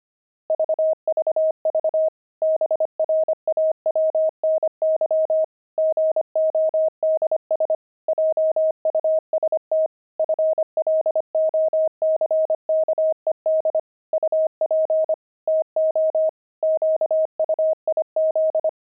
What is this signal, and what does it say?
Morse code, heard as a keyed tone, about 25 wpm, 630 Hz. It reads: VVV BRAWNY GODS JUST FLOCKED UP TO QUIZ
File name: pangram-fs-25-cs-25